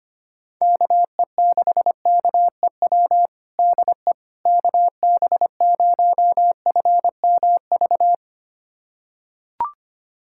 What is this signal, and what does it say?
Morse code, heard as a keyed tone, about 25 wpm, 700 Hz. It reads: KE6KEW DE KB0FMV
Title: USA Callsign DE Callsign 25wpm